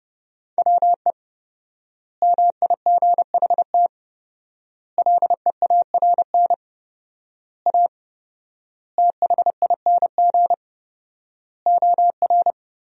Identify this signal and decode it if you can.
Morse code, heard as a keyed tone, about 32 wpm, 700 Hz. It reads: WE MIGHT LEARN A THING OR
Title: 4x Word-Spacing - Sentences from Top 300 Words - 30wpm